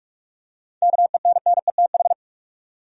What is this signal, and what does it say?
Morse code, heard as a keyed tone, about 46 wpm, 700 Hz. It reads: KENNETH
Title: USA Common Names - 45wpm